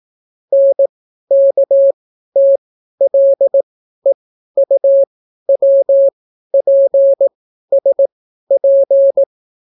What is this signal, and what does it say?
Morse code, heard as a keyed tone, about 18 wpm, spaced as for 12 wpm, 550 Hz. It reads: NKTLEUWPSP